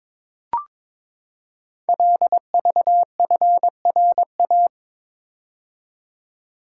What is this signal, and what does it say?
Morse code, heard as a keyed tone, about 22 wpm, 700 Hz. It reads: LVFRA
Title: Code Groups - 5 Letters - 22wpm